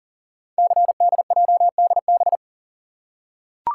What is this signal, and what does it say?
Morse code, heard as a keyed tone, about 40 wpm, 700 Hz. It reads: CDJDB
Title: Code Groups - 5 Letters - 40wpm